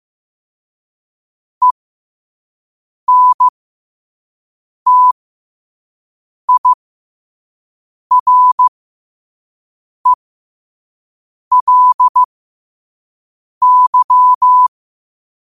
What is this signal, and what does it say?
Morse code, heard as a keyed tone, about 15 wpm, spaced as for 5 wpm, 1000 Hz. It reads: ENTIRELY
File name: qotdfiles / 5wpm